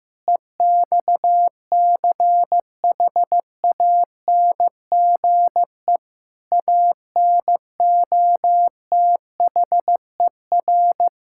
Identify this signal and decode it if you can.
Morse code, heard as a keyed tone, about 15 wpm, 700 Hz. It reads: EXCHANGE ANOTHER
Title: Sentences from Top 1000 Words - Review Part 6 - 15wpm